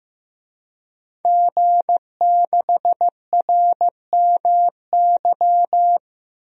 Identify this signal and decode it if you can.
Morse code, heard as a keyed tone, about 15 wpm, 700 Hz. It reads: G6RMY
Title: DXCC-100 Call Signs - 5 Characters - Repeated 2x - 15wpm